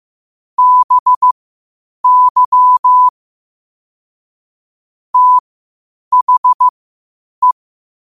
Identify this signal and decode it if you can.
Morse code, heard as a keyed tone, about 15 wpm, spaced as for 8 wpm, 1000 Hz. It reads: BY THE
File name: qotdfiles / 8wpm